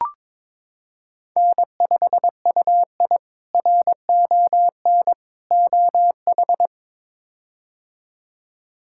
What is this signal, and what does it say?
Morse code, heard as a keyed tone, about 21 wpm, 700 Hz. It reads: N5UI RON OH
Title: K1USN SST QSO Practice - 22wpm